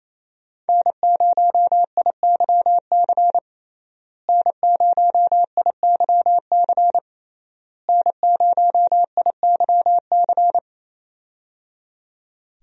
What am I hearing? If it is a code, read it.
Morse code, heard as a keyed tone, about 28 wpm, 700 Hz. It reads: N0IYC N0IYC N0IYC
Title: USA Call Signs - 5 Characters - Repeated 3x - 28wpm